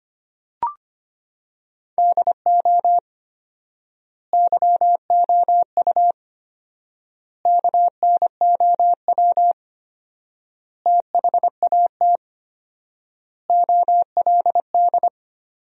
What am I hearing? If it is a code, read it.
Morse code, heard as a keyed tone, about 25 wpm, 700 Hz. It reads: DO YOU KNOW THAT OLD
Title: 4x Word-Spacing - Sentences from Top 400 Words - 25wpm